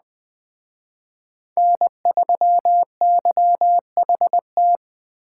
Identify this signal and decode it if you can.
Morse code, heard as a keyed tone, about 20 wpm, 700 Hz. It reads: N3YHT
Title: Speed Racing - USA 1x3 Call Signs 17wpm